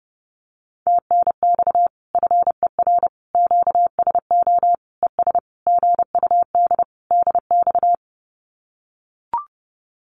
Morse code, heard as a keyed tone, about 28 wpm, 700 Hz. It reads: TNX FER QSO ES GUD DX
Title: Common QSO Phrases Encore 30wpm